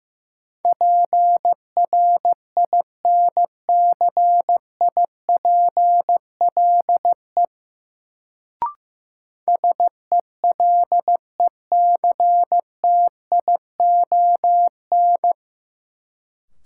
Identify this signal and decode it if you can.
Morse code, heard as a keyed tone, about 15 wpm, 700 Hz. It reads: PRINCIPLE SELECTION
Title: Sentences from Top 1400 Words - New Word Drill Alt - 15wpm